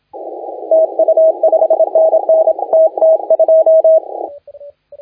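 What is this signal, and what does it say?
Morse code, heard as a keyed tone, about 27 wpm, 640 Hz, with radio noise. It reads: TU 5NN TT2
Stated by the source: Morse contest audio recorded by LA4C during the SAC CW 2018 contest.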